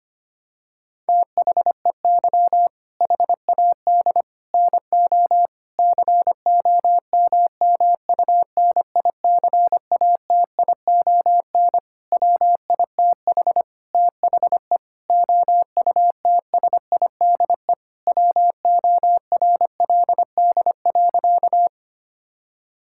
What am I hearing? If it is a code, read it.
Morse code, heard as a keyed tone, about 25 wpm, 700 Hz. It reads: THEY HAD NO COMMUNICATION WITH THE OUTSIDE WORLD.
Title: Sentences from Top 1000 Words - Review Part 6 - 25wpm